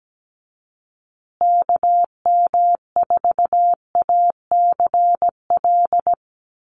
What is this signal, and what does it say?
Morse code, heard as a keyed tone, about 17 wpm, 700 Hz. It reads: KM4ACL
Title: USA Call Signs Repeated 3x 17wpm